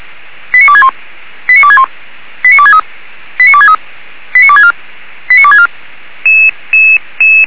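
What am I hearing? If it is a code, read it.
Five-tone selective call (ZVEI), heard as quick runs of five tones, tones of about 70 ms. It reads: 89261 89261 89263 89263 89264 89264
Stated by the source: ZVEI five-tone call received by SDR, from feuerwehr-orlamuende.org fire brigade audio archive.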